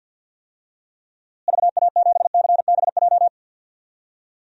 Morse code, heard as a keyed tone, about 50 wpm, 700 Hz. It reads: VA7CBJ
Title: DXCC-100 Call Signs - 6 Characters - Repeated 3x - 50wpm